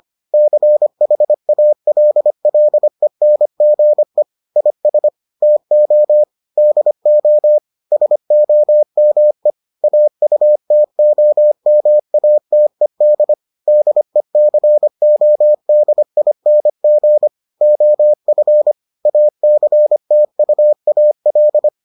Morse code, heard as a keyed tone, about 25 wpm, 600 Hz. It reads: CHALLENGE IS TO DO SOME AUTOMATED DECODING OF ACTUAL